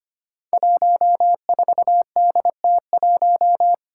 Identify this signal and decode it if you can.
Morse code, heard as a keyed tone, about 25 wpm, 700 Hz. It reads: 14DT1
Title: Five Challenging Letter-Number Combinations 25wpm